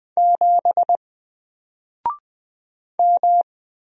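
Morse code, heard as a keyed tone, about 20 wpm, 700 Hz. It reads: 7 M
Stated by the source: Single letter and number Morse drill at 20 wpm, from N4LCD.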